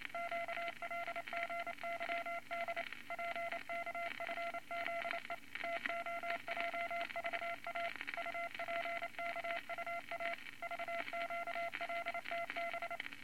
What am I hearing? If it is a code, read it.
Morse code, heard as a keyed tone, about 29 wpm, 680 Hz, with noise. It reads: QLGYD PKFZE TOJVA UPKUA VQLTB
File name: Morse Code Records / 22 A LCM 172 Twenty Words Per Minute